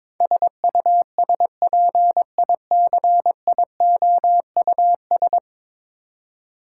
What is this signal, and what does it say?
Morse code, heard as a keyed tone, about 21 wpm, 700 Hz. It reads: SUSPICIOUS
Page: morse-code-ninja-practice-22wpm